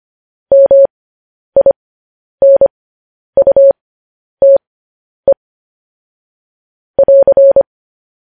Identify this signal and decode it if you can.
Morse code, heard as a keyed tone, about 25 wpm, spaced as for 10 wpm, 550 Hz. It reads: MINUTE <AR>
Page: news headlines in morse code at 10 wpm